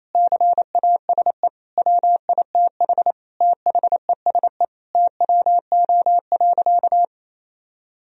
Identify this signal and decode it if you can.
Morse code, heard as a keyed tone, about 28 wpm, 700 Hz. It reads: CASE WITH THESE TWO.
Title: Sentences from Top 1000 Words - Review Part 1 - 28wpm